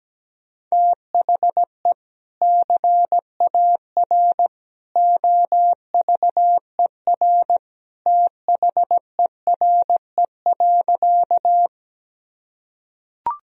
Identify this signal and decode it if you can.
Morse code, heard as a keyed tone, about 17 wpm, 700 Hz. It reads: THE CAR OVER THERE.
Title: Sentences from Top 1000 Words - Review Part 1 - 17wpm